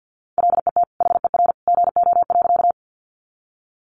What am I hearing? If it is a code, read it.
Morse code, heard as a keyed tone, about 50 wpm, 700 Hz. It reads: LET HER GO.
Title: Sentences from Top 300 Words 50wpm